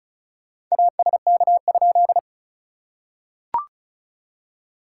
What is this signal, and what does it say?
Morse code, heard as a keyed tone, about 36 wpm, 690 Hz. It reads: ASK?
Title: SOTA QSO Practice 35wpm